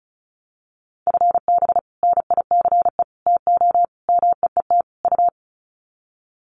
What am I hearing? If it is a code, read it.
Morse code, heard as a keyed tone, about 34 wpm, 700 Hz. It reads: FB NICE TO MEET U
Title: Common QSO Phrases 35wpm